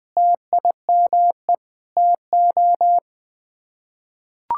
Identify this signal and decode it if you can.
Morse code, heard as a keyed tone, about 21 wpm, 700 Hz. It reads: TIME TO
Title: Build A Phrase - 5-Word Phrases - 20wpm